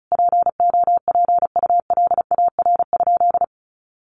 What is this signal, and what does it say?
Morse code, heard as a keyed tone, about 34 wpm, 700 Hz. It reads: POPULAR?
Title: Sentences from Top 1200 Words 35wpm